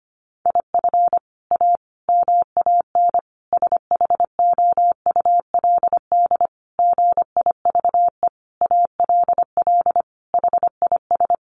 Morse code, heard as a keyed tone, about 25 wpm, 700 Hz. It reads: IF A MAN SHOULD GIVE ALL HIS
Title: Sentences from Top 1500 Words 25wpm